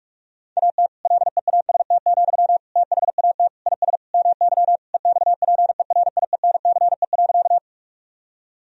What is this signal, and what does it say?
Morse code, heard as a keyed tone, about 47 wpm, 700 Hz. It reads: AT LEAST, THAT IS MY EXPERIENCE.
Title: Sentences from Top 1000 Words - Review Part 3 - 45wpm